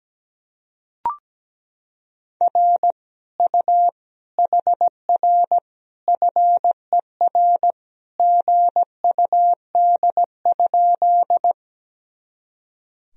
Morse code, heard as a keyed tone, about 17 wpm, 700 Hz. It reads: R U HR FER GUD?
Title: Sentences from Top 200 Words and Abbreviations 17wpm